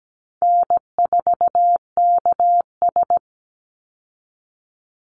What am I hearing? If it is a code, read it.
Morse code, heard as a keyed tone, about 17 wpm, 700 Hz. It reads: N4KS
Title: Easy Call Signs Repeated 3x - 17wpm